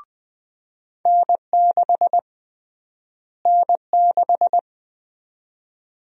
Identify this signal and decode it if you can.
Morse code, heard as a keyed tone, about 20 wpm, 700 Hz. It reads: N6 N6
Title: USA Call Sign Prefixes - Repeated 2x - 20wpm